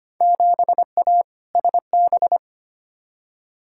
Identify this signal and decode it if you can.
Morse code, heard as a keyed tone, about 25 wpm, 700 Hz. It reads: 7A SB
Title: Field Day Exchanges Encore - 25wpm